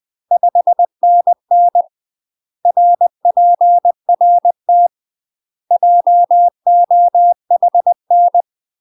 Morse code, heard as a keyed tone, about 20 wpm, 700 Hz. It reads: 5NN RPRT JOHN
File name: UK MASTER QSOs TRAINING / 20WPM / UK MASTER QSOs 0001